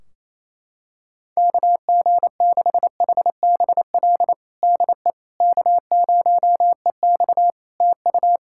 Morse code, heard as a keyed tone, about 28 wpm, 700 Hz. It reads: KG6HBL DE K0EX TU
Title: K1USN SST QSO Practice 28wpm